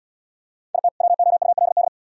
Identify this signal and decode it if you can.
Morse code, heard as a keyed tone, about 78 wpm, 700 Hz. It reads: IT WOULD